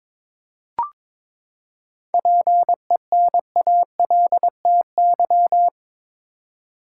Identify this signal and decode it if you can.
Morse code, heard as a keyed tone, about 22 wpm, 700 Hz. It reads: PENALTY
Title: Seven Letter Words - 22wpm